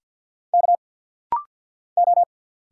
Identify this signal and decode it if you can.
Morse code, heard as a keyed tone, about 50 wpm, 690 Hz. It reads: K O
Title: Single Letter-Number - Rapid-Fire 50wpm